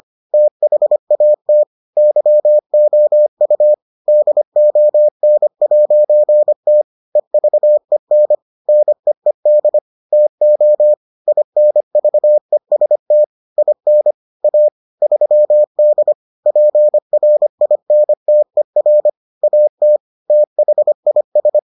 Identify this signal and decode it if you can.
Morse code, heard as a keyed tone, about 25 wpm, 600 Hz. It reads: THAT YOU DON'T EVEN NEED TO INVEST IN A 3D PRINTER AT THIS